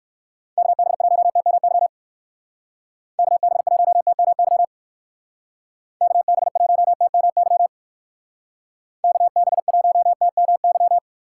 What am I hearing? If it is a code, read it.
Morse code, heard as a keyed tone, about 54 wpm, 700 Hz. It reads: KB1TMY KB1TMY KB1TMY KB1TMY
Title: Speed Racing - USA 2x3 Call Signs 45wpm